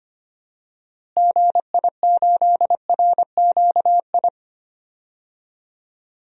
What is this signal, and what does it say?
Morse code, heard as a keyed tone, about 25 wpm, 700 Hz. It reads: GI8RQI
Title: DXCC-100 Call Signs - 6 Characters - Repeated 2x - 25wpm